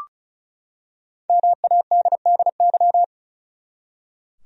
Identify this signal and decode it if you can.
Morse code, heard as a keyed tone, about 36 wpm, 700 Hz. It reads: MADDY
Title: USA Common Names - 35wpm